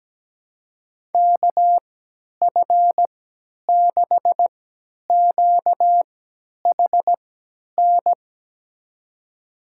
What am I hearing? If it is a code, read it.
Morse code, heard as a keyed tone, about 17 wpm, spaced as for 10 wpm, 700 Hz. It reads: KF6QHN
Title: USA Call Signs - 6 Characters - 3x Character Spacing - 17wpm